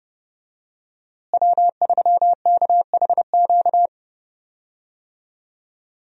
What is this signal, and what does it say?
Morse code, heard as a keyed tone, about 31 wpm, 700 Hz. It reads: W3KHQ
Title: USA Call Signs - 5 Characters - Repeated 2x - 30wpm